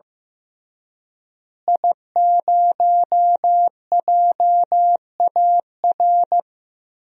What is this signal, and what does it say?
Morse code, heard as a keyed tone, about 15 wpm, 700 Hz. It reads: I0JAR
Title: DXCC-100 Call Signs - 5 Characters - Repeated 3x - 15wpm